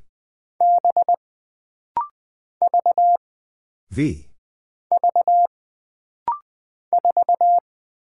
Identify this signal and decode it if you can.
Morse code, heard as a keyed tone, about 20 wpm, 700 Hz. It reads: B V V 4
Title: Single Character (Letter:Number:Punctuation:Prosign) - Rapid-Fire - 20wpm